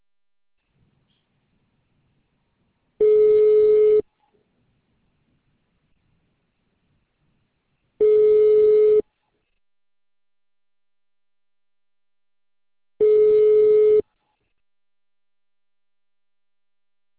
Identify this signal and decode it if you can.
Ringback tone (Europe, 425 Hz).